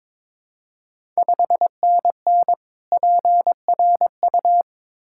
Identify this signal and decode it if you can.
Morse code, heard as a keyed tone, about 22 wpm, 700 Hz. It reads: 5NN PRU
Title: Ontario QSO Party Exchange 22wpm